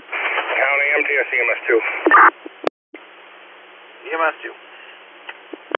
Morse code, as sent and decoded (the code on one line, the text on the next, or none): none